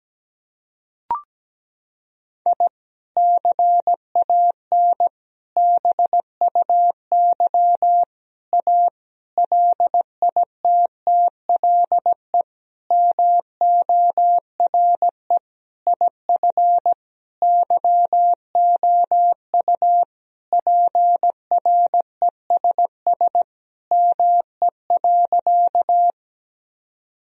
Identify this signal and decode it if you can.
Morse code, heard as a keyed tone, about 17 wpm, 700 Hz. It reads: I CAN BUY A LITTLE MORE IF YOU PRESS ME.
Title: Sentences from Top 1000 Words - Review Part 1 - 17wpm